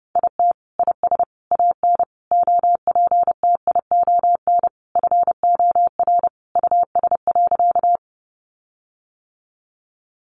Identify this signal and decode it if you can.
Morse code, heard as a keyed tone, about 30 wpm, 700 Hz. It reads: IT IS AN OPTION FOR US.
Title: Sentences from Top 1300 Words 30wpm